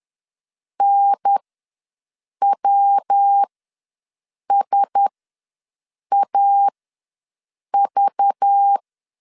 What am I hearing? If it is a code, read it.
Morse code, heard as a keyed tone, about 10 wpm, spaced as for 6 wpm, 800 Hz. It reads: NWSAV
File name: LEZIONI / 10 SIGNS / 10B INT ws 10wpm 4134